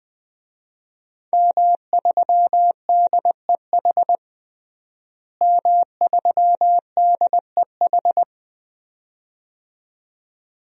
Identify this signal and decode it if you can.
Morse code, heard as a keyed tone, about 20 wpm, 700 Hz. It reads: M3DEH M3DEH
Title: DXCC-100 Call Signs - 5 Characters - Repeated 2x - 20wpm